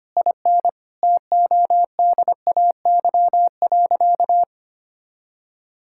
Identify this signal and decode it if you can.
Morse code, heard as a keyed tone, about 25 wpm, 700 Hz. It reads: IN TODAY.
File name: practice / 25wpm / Sentences with 4-Word Phrases - 25wpm